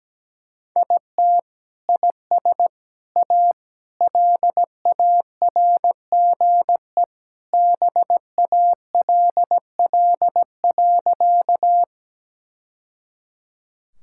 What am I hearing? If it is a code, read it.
Morse code, heard as a keyed tone, about 17 wpm, 700 Hz. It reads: IT IS A LARGE BALL.
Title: Sentences from Top 1000 Words - Review Part 6 - 17wpm